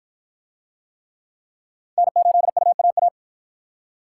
Morse code, heard as a keyed tone, about 53 wpm, 700 Hz. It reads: N8UAA